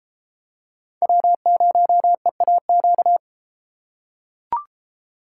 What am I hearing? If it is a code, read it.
Morse code, heard as a keyed tone, about 33 wpm, 700 Hz. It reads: W0EAQ